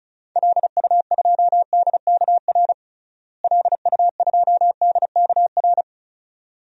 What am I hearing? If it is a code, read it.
Morse code, heard as a keyed tone, about 36 wpm, 700 Hz. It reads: LU2DKR LU2DKR
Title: DXCC-100 Call Signs - 6 Characters - Repeated 2x - 35wpm